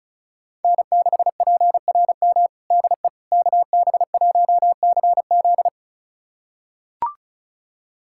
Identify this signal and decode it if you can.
Morse code, heard as a keyed tone, about 34 wpm, 700 Hz. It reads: N6PRM DE KB1CZ
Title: USA Callsign DE Callsign - 5 Characters - 35wpm